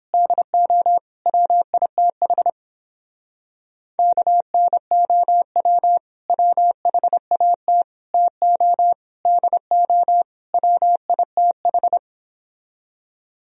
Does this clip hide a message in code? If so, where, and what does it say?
Yes: Morse code from 0.1 to 12.0 s, heard as a keyed tone, about 29 wpm, 700 Hz. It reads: DO WITH KNOW WHAT TO DO WITH